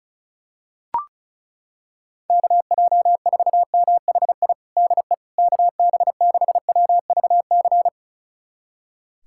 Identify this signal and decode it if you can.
Morse code, heard as a keyed tone, about 34 wpm, 700 Hz. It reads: KJ4MHI DE KB6WVC
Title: USA Callsign DE Callsign - 6 Characters - 35wpm